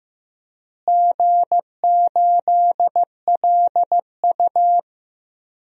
Morse code, heard as a keyed tone, about 16 wpm, 700 Hz. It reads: G8LU
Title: DXCC-100 Call Signs - 4 Characters - Repeated 3x - 15wpm